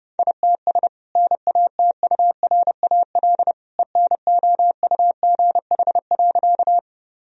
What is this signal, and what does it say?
Morse code, heard as a keyed tone, about 30 wpm, 700 Hz. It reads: ITS NATURAL ENOUGH.
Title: Sentences from Top 1000 Words - Review Part 4 - 30wpm